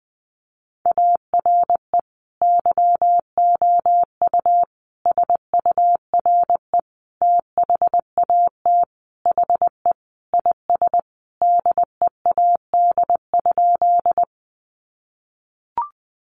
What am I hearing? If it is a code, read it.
Morse code, heard as a keyed tone, about 20 wpm, 700 Hz. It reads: ARE YOU SURE THAT HE IS DEAD?
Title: Sentences from Top 1500 Words 20wpm